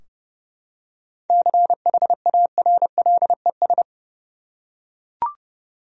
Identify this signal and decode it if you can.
Morse code, heard as a keyed tone, about 30 wpm, 700 Hz. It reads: CHARLES
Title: USA Common Names - 30wpm